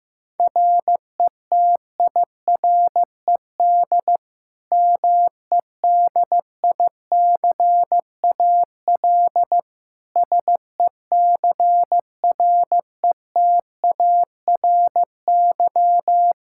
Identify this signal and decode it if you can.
Morse code, heard as a keyed tone, about 15 wpm, 700 Hz. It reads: RETIRED MEDICAL SECRETARY
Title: K1USN SST QSO Practice 15wpm